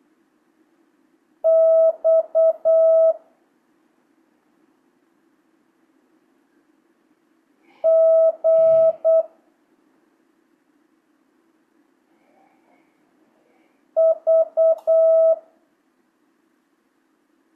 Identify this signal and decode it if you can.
Morse code, heard as a keyed tone, about 8 wpm, 650 Hz. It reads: X G V
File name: jh8cbh / 2025cw